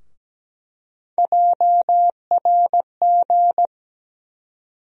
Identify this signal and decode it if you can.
Morse code, heard as a keyed tone, about 17 wpm, 700 Hz. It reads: JRG
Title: USA Call Sign Postfixes - 3 Letters 17wpm